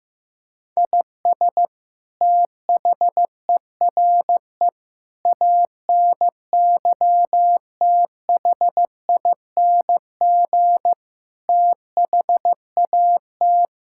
Morse code, heard as a keyed tone, about 15 wpm, 700 Hz. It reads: IS THERE ANYTHING THAT
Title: Sentences from Top 1000 Words - Review Part 6 - 15wpm